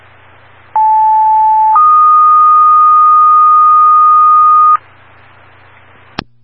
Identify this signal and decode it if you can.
Two-tone page, heard as a one-second tone, then a three-second tone, about 870 then 1230 Hz. It is heard from 0.7 to 4.8 s.